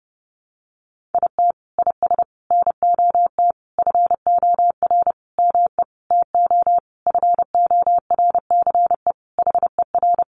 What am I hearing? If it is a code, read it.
Morse code, heard as a keyed tone, about 30 wpm, 700 Hz. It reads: IT IS NOT FOR ME TO FORCE HER
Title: Sentences from Top 1500 Words 30wpm